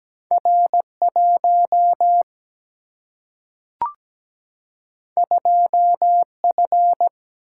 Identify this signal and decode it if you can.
Morse code, heard as a keyed tone, about 17 wpm, 700 Hz. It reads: R1 2F
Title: Single-Letter Single-Number 17wpm